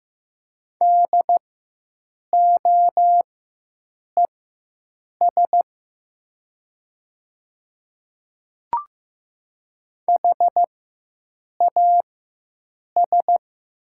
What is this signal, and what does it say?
Morse code, heard as a keyed tone, about 15 wpm, spaced as for 7 wpm, 700 Hz. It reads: DOES HAS
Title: 4x Character Spacing - Top 200 Words - 15wpm